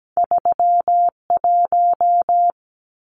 Morse code, heard as a keyed tone, about 17 wpm, 700 Hz. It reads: 31
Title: Numbers - 2-Digit 17wpm